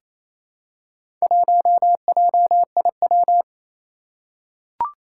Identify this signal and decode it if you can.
Morse code, heard as a keyed tone, about 28 wpm, 700 Hz. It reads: 1JIW
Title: Four Challenging Letter-Number Combinations 28wpm